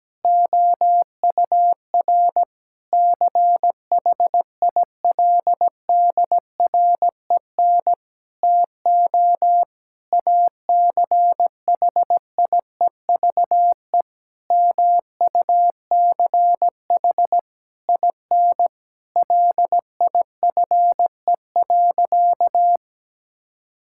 Morse code, heard as a keyed tone, about 17 wpm, 700 Hz. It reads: OUR CHILDREN TO ACHIEVE MUCH IN LIFE.
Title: Sentences from Top 1000 Words - Review Part 6 - 17wpm